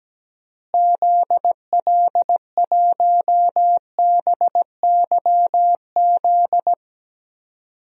Morse code, heard as a keyed tone, about 17 wpm, 700 Hz. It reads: ZL1BYZ
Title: DXCC-100 Call Signs - 6 Characters - Repeated 2x - 17wpm